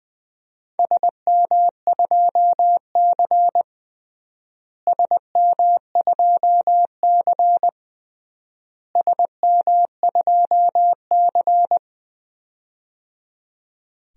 Morse code, heard as a keyed tone, about 20 wpm, 700 Hz. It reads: SM2C SM2C SM2C
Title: DXCC-100 Call Signs - 4 Characters - Repeated 3x - 20wpm